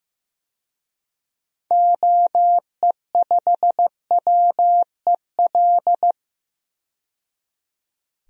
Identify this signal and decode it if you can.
Morse code, heard as a keyed tone, about 15 wpm, 700 Hz. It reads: OE5WEL
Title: DXCC-100 Call Signs - 6 Characters - 15wpm